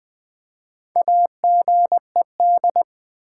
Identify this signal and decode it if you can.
Morse code, heard as a keyed tone, about 20 wpm, 700 Hz. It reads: AGED
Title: Build A Word - 1 Letter At A Time - 3 to 5-Letter Words - 20wpm